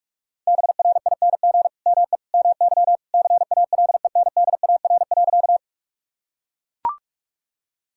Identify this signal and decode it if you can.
Morse code, heard as a keyed tone, about 41 wpm, 700 Hz. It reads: BRING ME MY CALENDAR.
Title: Sentences from Top 1600 Words 45wpm